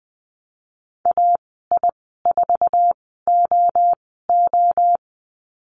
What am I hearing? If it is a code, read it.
Morse code, heard as a keyed tone, about 20 wpm, spaced as for 15 wpm, 700 Hz. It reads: AI4OO
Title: 048 Lesson 9g - Number 4 - Call Signs 20wpm